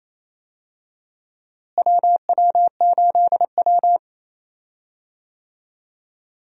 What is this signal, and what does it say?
Morse code, heard as a keyed tone, about 28 wpm, 700 Hz. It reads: WW8W
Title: USA Call Signs - 4 Characters - Repeated 3x - 28wpm